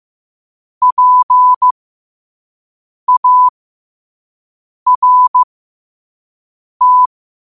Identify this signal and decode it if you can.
Morse code, heard as a keyed tone, about 15 wpm, spaced as for 5 wpm, 1000 Hz. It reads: PART
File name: qotdfiles / 5wpm